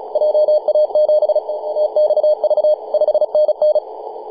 Morse code, heard as a keyed tone, about 34 wpm, 610 Hz. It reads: JA7TXV 5NN
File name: s79hn-24-910mhz-cw